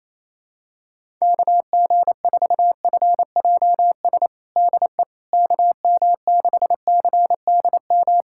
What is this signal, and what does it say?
Morse code, heard as a keyed tone, about 28 wpm, 700 Hz. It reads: KG4FJS DE KM6CDM
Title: USA Callsign DE Callsign 28wpm